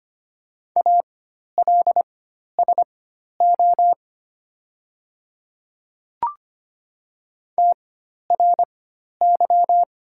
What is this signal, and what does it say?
Morse code, heard as a keyed tone, about 25 wpm, spaced as for 12 wpm, 700 Hz. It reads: ALSO TRY
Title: 4x Character Spacing - Top 500 Words - 25wpm